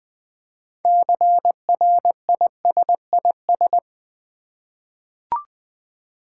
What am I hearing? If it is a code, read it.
Morse code, heard as a keyed tone, about 20 wpm, 700 Hz. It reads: CRISIS
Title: 6 Letter Words - First 3 Letters Given - 20wpm